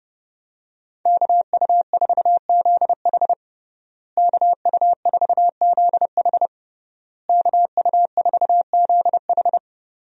Morse code, heard as a keyed tone, about 30 wpm, 700 Hz. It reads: KU4ZH KU4ZH KU4ZH
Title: USA Call Signs - 5 Characters - Repeated 3x - 30wpm